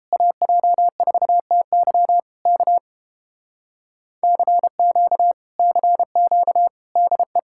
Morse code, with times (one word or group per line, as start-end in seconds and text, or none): 0.1-2.2 AJ4TY
2.5-2.8 K
4.2-5.3 CQ
5.6-6.7 CQ
7.0-7.4 DE